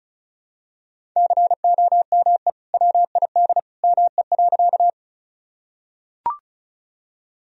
Morse code, heard as a keyed tone, about 34 wpm, 700 Hz. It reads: COME WID ME.
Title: Sentences from Top 200 Words and Abbreviations 35wpm